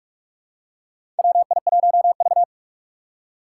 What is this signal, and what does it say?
Morse code, heard as a keyed tone, about 42 wpm, 690 Hz. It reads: WI1V
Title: USA Call Signs - 4 Characters - 8x Character Spacing - 45wpm